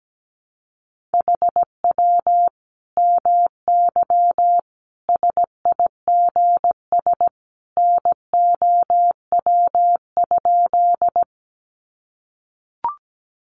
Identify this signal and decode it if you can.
Morse code, heard as a keyed tone, about 17 wpm, 700 Hz. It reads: HW MY SIGS NOW?
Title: Common QSO Phrases 17wpm